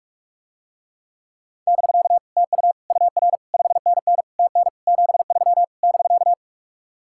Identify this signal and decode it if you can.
Morse code, heard as a keyed tone, about 44 wpm, 700 Hz. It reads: <BK> TU UR 5NN TN 73 <BK>